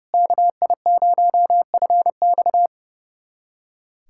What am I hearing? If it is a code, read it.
Morse code, heard as a keyed tone, about 30 wpm, 700 Hz. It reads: KI0FX
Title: USA Callsign DE Callsign 30wpm